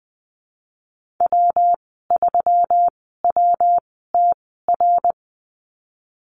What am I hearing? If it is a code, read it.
Morse code, heard as a keyed tone, about 20 wpm, spaced as for 14 wpm, 700 Hz. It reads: W3WTR
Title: 122 Lesson 20g - Number 3 - Call Signs 20wpm